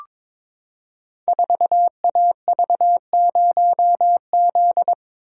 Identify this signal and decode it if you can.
Morse code, heard as a keyed tone, about 22 wpm, 700 Hz. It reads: 4AV0Z
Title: Code Groups - 5 Letter:Numbers - 22wpm